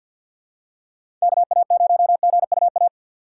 Morse code, heard as a keyed tone, about 48 wpm, 700 Hz. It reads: KA0GUA
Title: USA Call Signs - 6 Characters - 6x Character Spacing - 50wpm